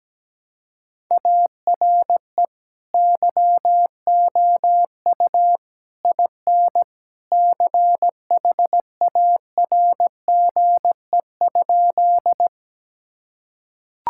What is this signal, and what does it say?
Morse code, heard as a keyed tone, about 17 wpm, 700 Hz. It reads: ARE YOU IN CHARGE?
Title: Sentences from Top 1000 Words - Review Part 6 - 17wpm